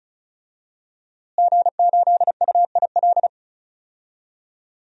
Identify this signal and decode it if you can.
Morse code, heard as a keyed tone, about 34 wpm, 700 Hz. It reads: G8UIL
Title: DXCC-100 Call Signs - 5 Characters - Repeated 3x - 35wpm